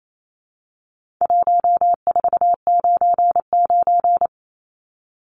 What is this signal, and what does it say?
Morse code, heard as a keyed tone, about 28 wpm, 700 Hz. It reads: 1499
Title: Numbers - 4-Digit 28wpm